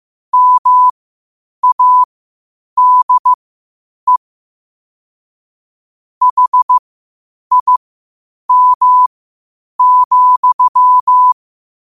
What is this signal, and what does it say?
Morse code, heard as a keyed tone, about 15 wpm, spaced as for 8 wpm, 1000 Hz. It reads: MADE HIM,
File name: qotdfiles / 8wpm